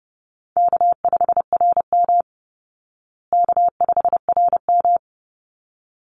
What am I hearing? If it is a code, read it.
Morse code, heard as a keyed tone, about 30 wpm, 700 Hz. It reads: K5RM K5RM
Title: Easy Call Signs Repeated 2x - 30wpm